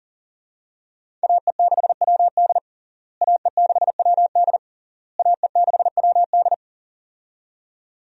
Morse code, heard as a keyed tone, about 37 wpm, 700 Hz. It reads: AE6WD AE6WD AE6WD
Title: USA Call Signs - 5 Characters - Repeated 3x - 40wpm